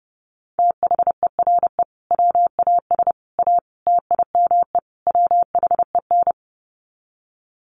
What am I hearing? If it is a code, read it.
Morse code, heard as a keyed tone, about 30 wpm, 700 Hz. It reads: THERE WAS A TIME WHEN
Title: Sets of 5 Words 30wpm